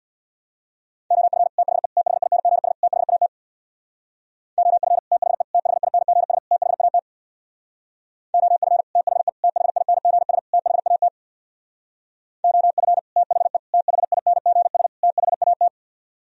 Morse code, heard as a keyed tone, about 64 wpm, 700 Hz. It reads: OF THE THINGS THAT OF THE THINGS THAT OF THE THINGS THAT OF THE THINGS THAT